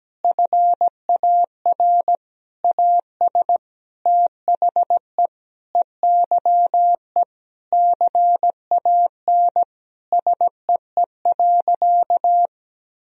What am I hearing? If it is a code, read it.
Morse code, heard as a keyed tone, about 17 wpm, 700 Hz. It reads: FAR AS THE EYE CAN SEE.
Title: Sentences from Top 1000 Words - Review Part 1 - 17wpm